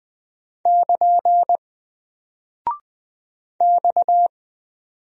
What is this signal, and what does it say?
Morse code, heard as a keyed tone, about 20 wpm, 700 Hz. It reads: <KN> X
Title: Single Character (Letter:Number:Punctuation:Prosign) Alt 20wpm